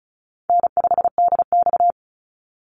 Morse code, heard as a keyed tone, about 34 wpm, 700 Hz. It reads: N5DX
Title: Easy Call Signs Repeated 3x - 35wpm